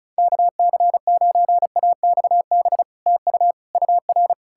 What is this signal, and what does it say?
Morse code, heard as a keyed tone, about 46 wpm, spaced as for 41 wpm, 700 Hz. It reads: KC9AXB TU UR
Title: Speed-Racing Common QSO Phrases - 35wpm